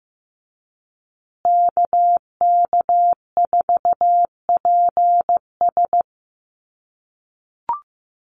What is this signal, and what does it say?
Morse code, heard as a keyed tone, about 15 wpm, 700 Hz. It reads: KK4PS
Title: USA Call Signs Repeated 3x 15wpm